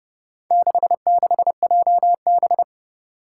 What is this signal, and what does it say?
Morse code, heard as a keyed tone, about 30 wpm, 700 Hz. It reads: 66JB
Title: Four Challenging Letter-Number Combinations 30wpm